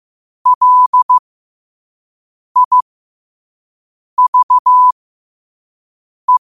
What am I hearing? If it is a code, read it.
Morse code, heard as a keyed tone, about 15 wpm, spaced as for 5 wpm, 1000 Hz. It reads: LIVE
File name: qotdfiles / 5wpm